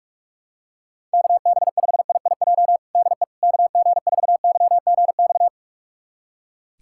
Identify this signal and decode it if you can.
Morse code, heard as a keyed tone, about 46 wpm, 700 Hz. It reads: KB5IIJ DE KG4YGX
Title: USA Callsign DE Callsign - 6 Characters - 45wpm